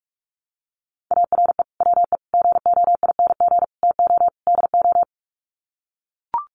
Morse code, heard as a keyed tone, about 44 wpm, 700 Hz. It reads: ARE WE GOING TO DO
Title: Sets of 5 Words 45wpm